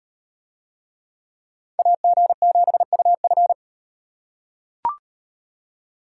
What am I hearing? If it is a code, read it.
Morse code, heard as a keyed tone, about 36 wpm, 700 Hz. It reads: AG7UF